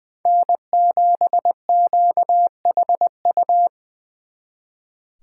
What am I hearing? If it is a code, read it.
Morse code, heard as a keyed tone, about 24 wpm, 700 Hz. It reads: N7QHU
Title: Speed Racing - USA 1x3 Call Signs 20wpm